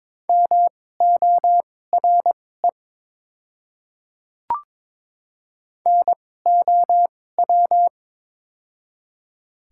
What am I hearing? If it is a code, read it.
Morse code, heard as a keyed tone, about 22 wpm, spaced as for 16 wpm, 700 Hz. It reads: MORE NOW
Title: 2x Character Spacing - Top 100 Words - 22wpm